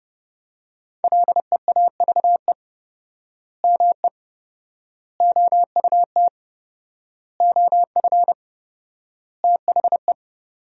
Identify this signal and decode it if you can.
Morse code, heard as a keyed tone, about 30 wpm, 700 Hz. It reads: LEAVE ME OUT OF THE
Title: 4x Word-Spacing - Sentences from Top 400 Words - 30wpm